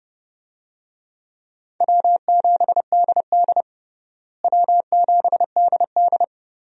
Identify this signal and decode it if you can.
Morse code, heard as a keyed tone, about 30 wpm, 700 Hz. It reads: W7DD W7DD
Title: USA Call Signs - 4 Characters - Repeated 3x - 30wpm